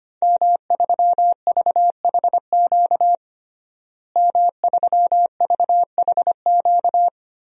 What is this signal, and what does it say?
Morse code, heard as a keyed tone, about 25 wpm, 700 Hz. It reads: M3VHQ M3VHQ
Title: DXCC-100 Call Signs - 5 Characters - Repeated 2x - 25wpm